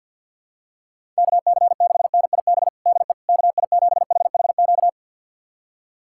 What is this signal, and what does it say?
Morse code, heard as a keyed tone, about 51 wpm, 700 Hz. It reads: KC6NIB DE KI7HHQ
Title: USA Callsign DE Callsign - 6 Characters - 50wpm